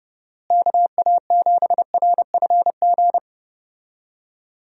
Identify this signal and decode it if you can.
Morse code, heard as a keyed tone, about 32 wpm, 700 Hz. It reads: KA7RFG
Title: USA Call Signs - 6 Characters - Repeated 2x - 30wpm